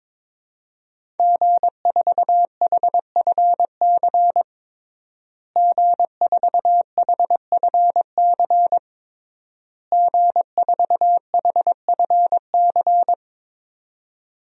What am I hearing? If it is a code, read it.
Morse code, heard as a keyed tone, about 22 wpm, 700 Hz. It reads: G4HFC G4HFC G4HFC
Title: DXCC-100 Call Signs - 5 Characters - Repeated 3x - 22wpm